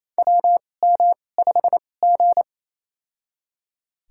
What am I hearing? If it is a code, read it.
Morse code, heard as a keyed tone, about 28 wpm, spaced as for 20 wpm, 700 Hz. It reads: WM5G
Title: USA Call Signs - 4 Characters - 2x Character Spacing - 28wpm